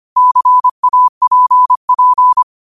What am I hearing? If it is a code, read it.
Morse code, heard as a keyed tone, about 25 wpm, 1000 Hz. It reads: CAPP
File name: qotdfiles / 25wpm